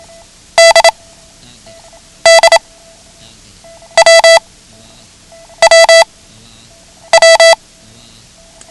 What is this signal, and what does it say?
Morse code, heard as a keyed tone, about 27 wpm, 660 Hz. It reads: D D W W W
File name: ham radio / CodeQuick / Code Quick 09